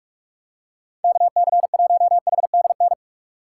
Morse code, heard as a keyed tone, about 41 wpm, 700 Hz. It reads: KC1HDN
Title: USA Call Signs - 6 Characters - 2x Character Spacing - 45wpm